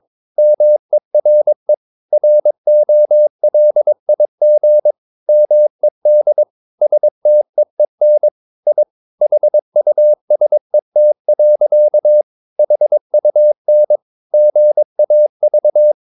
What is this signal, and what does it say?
Morse code, heard as a keyed tone, about 22 wpm, 600 Hz. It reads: MERE ROLIG MED STEEN I HUSET. HUN GAV
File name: Ferie QRS 22wpm